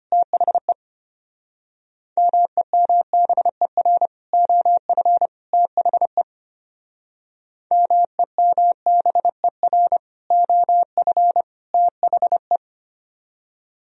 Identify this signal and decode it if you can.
Morse code, heard as a keyed tone, about 33 wpm, 700 Hz. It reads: THE MEMBER OF THE MEMBER OF THE